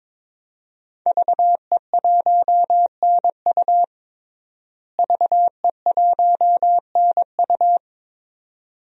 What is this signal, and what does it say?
Morse code, heard as a keyed tone, about 22 wpm, 700 Hz. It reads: VE1NU VE1NU
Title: DXCC-100 Call Signs - 5 Characters - Repeated 2x - 22wpm